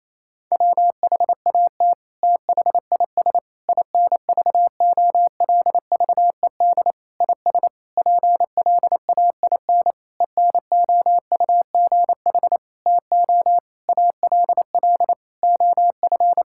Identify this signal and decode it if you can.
Morse code, heard as a keyed tone, about 28 wpm, 700 Hz. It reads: WHAT THIS INVOLVED IS PLAIN ENOUGH TO ALL OF
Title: Sentences from Top 1000 Words - Review Part 5 - 28wpm